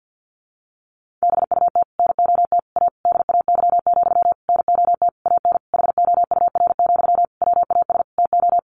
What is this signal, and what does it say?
Morse code, heard as a keyed tone, about 50 wpm, 700 Hz. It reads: BUT NOT A DAY, NOT AN HOUR, WAS TO
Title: Sentences from Top 1400 Words 50wpm